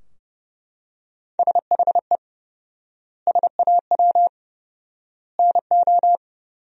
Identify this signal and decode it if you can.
Morse code, heard as a keyed tone, about 30 wpm, 700 Hz. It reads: SHE SAW NO
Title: 4x Word-Spacing - Sentences from Top 300 Words - 30wpm